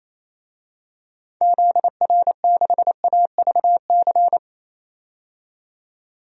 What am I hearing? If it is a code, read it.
Morse code, heard as a keyed tone, about 28 wpm, 700 Hz. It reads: ZR6AVC
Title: DXCC-100 Call Signs - 6 Characters - 28wpm